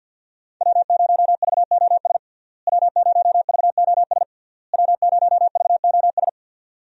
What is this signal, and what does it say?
Morse code, heard as a keyed tone, about 52 wpm, 700 Hz. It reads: W0VOS W0VOS W0VOS
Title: USA Call Signs - 5 Characters - Repeated 3x - 50wpm